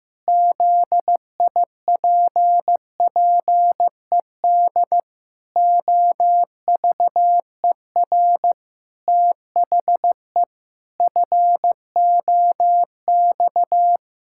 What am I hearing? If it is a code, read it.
Morse code, heard as a keyed tone, about 14 wpm, 700 Hz. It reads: ZIPPED OVER THE FOX
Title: Pangram Challenge v2 - 15wpm